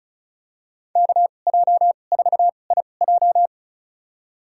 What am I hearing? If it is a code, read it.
Morse code, heard as a keyed tone, about 36 wpm, spaced as for 25 wpm, 700 Hz. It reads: KJ4IJ
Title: USA Call Signs - 5 Characters - 2x Character Spacing - 35wpm